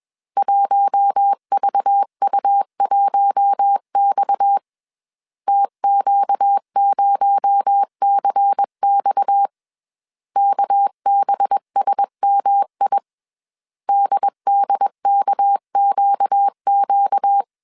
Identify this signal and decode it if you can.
Morse code, heard as a keyed tone, about 21 wpm, 800 Hz. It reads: 1VU1X TQ0CX KBSMI DDKQQ
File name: CW / DET / 1 6 1500C S C 20wpm 2116